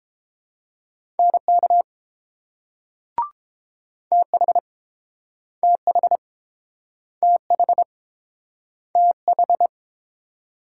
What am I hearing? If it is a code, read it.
Morse code, heard as a keyed tone, about 30 wpm, 700 Hz. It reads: NK TH TH TH TH